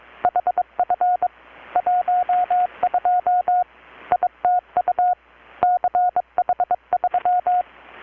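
Morse code, heard as a keyed tone, about 22 wpm, 720 Hz. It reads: HF 12 ITU CH3
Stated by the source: Morse from KPH's Night of Nights XX maritime radiotelegraphy event, received in Chiapas, Mexico about 3,800 km away on a Kenwood R-600 with a grounded 10 m long-wire antenna.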